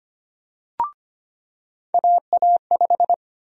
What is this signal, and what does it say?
Morse code, heard as a keyed tone, about 25 wpm, 700 Hz. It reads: AA5
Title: USA Call Sign Prefixes - 25wpm